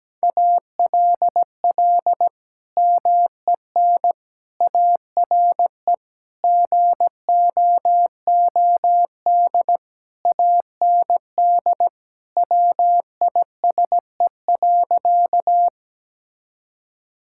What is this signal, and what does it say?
Morse code, heard as a keyed tone, about 17 wpm, 700 Hz. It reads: ALL MEN ARE GOOD AND WISE.
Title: Sentences from Top 1600 Words 17wpm